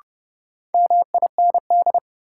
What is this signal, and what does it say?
Morse code, heard as a keyed tone, about 30 wpm, 700 Hz. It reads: MIND
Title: Sets of 5 Words - New Word Drill - 30wpm